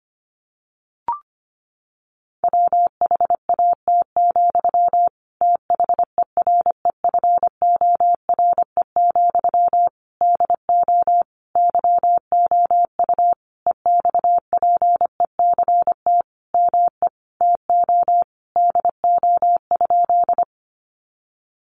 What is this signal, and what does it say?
Morse code, heard as a keyed tone, about 25 wpm, 700 Hz. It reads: WHAT, THEREFORE, DO YOU EXPECT ME TO DO?
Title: Sentences from Top 1100 Words 25wpm